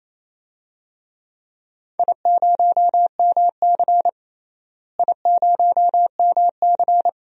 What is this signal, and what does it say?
Morse code, heard as a keyed tone, about 28 wpm, 700 Hz. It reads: I0MC I0MC
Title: DXCC-100 Call Signs - 4 Characters - Repeated 3x - 28wpm